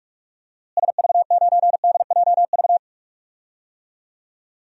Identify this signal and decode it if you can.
Morse code, heard as a keyed tone, about 46 wpm, 700 Hz. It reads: SV9DJV
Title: DXCC-100 Call Signs - 6 Characters - Repeated 2x - 45wpm